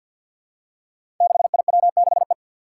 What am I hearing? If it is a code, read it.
Morse code, heard as a keyed tone, about 47 wpm, 700 Hz. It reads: 6IW6E
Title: Five Challenging Letter-Number Combinations 50wpm